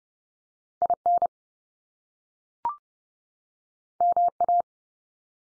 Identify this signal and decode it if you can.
Morse code, heard as a keyed tone, about 30 wpm, 700 Hz. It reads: IN MA
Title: US States 30WPM